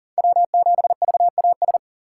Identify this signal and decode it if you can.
Morse code, heard as a keyed tone, about 40 wpm, 700 Hz. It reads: W7VAS
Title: K1USN SST QSO Practice - 40wpm